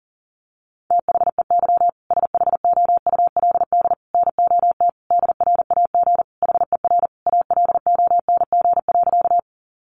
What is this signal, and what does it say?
Morse code, heard as a keyed tone, about 40 wpm, 700 Hz. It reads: THEY SHOULD NOT DRAG HER ALONG.
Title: Sentences from Top 1500 Words 40wpm